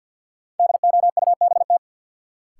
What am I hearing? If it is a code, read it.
Morse code, heard as a keyed tone, about 47 wpm, 700 Hz. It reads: DOUBT
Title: Sets of 5 Words - New Word Drill - 50wpm